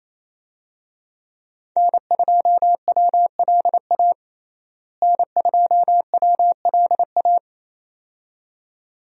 Morse code, heard as a keyed tone, about 27 wpm, 700 Hz. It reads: N2WLA N2WLA
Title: USA Call Signs - 5 Characters - Repeated 2x - 28wpm